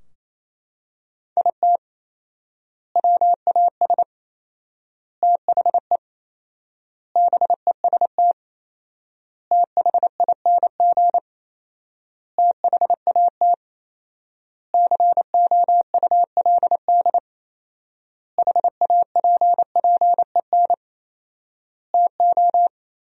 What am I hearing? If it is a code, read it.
Morse code, heard as a keyed tone, about 28 wpm, 700 Hz. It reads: IT WAS THE BEST THING THAT COULD HAPPEN TO
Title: 4x Word-Spacing - Sentences from Top 400 Words - 28wpm